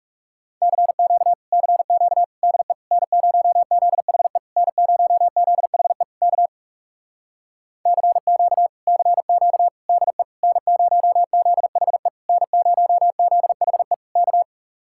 Morse code, heard as a keyed tone, about 38 wpm, 700 Hz. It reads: CQ CQ DE N0ZHE N0ZHE K CQ CQ DE N0ZHE N0ZHE K